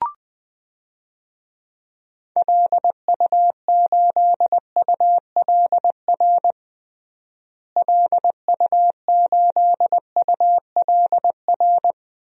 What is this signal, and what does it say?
Morse code, heard as a keyed tone, about 21 wpm, 700 Hz. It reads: LU8ULR LU8ULR
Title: DXCC-100 Call Signs - 6 Characters - Repeated 3x - 20wpm